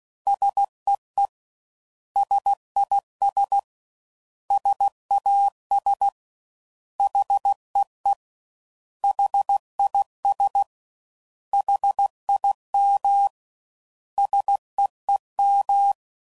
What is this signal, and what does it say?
Morse code, heard as a keyed tone, about 16 wpm, 800 Hz. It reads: SEE SIS SAS HEE HIS HIM SEEM
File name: Prefixes / By First Letter / S vs H